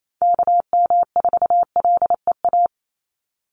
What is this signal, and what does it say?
Morse code, heard as a keyed tone, about 28 wpm, 700 Hz. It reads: KM4LEA
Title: USA Call Signs 28wpm